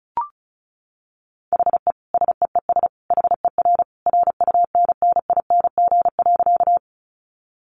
Morse code, heard as a keyed tone, about 34 wpm, 700 Hz. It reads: HE SEES HER RUNNING.
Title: Additional Easy Phrases 35wpm